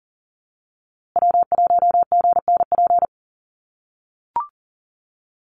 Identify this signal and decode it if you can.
Morse code, heard as a keyed tone, about 40 wpm, 700 Hz. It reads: W1GNP
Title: USA Call Signs Repeated 3x 40wpm